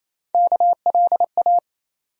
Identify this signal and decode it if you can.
Morse code, heard as a keyed tone, about 28 wpm, 700 Hz. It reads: KLA
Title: USA Call Sign Postfixes - 3 Letters 28wpm